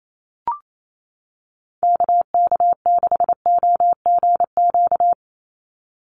Morse code, heard as a keyed tone, about 28 wpm, 700 Hz. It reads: KK6OGQ
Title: USA Call Signs 28wpm